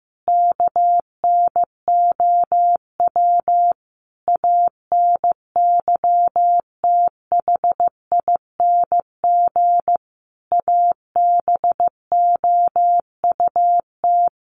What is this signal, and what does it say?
Morse code, heard as a keyed tone, about 15 wpm, 700 Hz. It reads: KNOW ANYTHING ABOUT
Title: Sets of 5 Words Encore2 15wpm